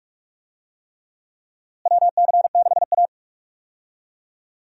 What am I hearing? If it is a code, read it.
Morse code, heard as a keyed tone, about 46 wpm, 700 Hz. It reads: WC6A
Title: USA Call Signs - 4 Characters - Repeated 3x - 45wpm